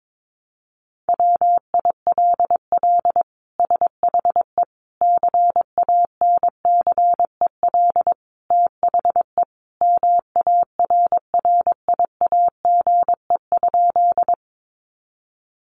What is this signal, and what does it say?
Morse code, heard as a keyed tone, about 22 wpm, 700 Hz. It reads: WILL SHE CANCEL THE MARRIAGE?
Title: Sentences from Top 1200 Words 22wpm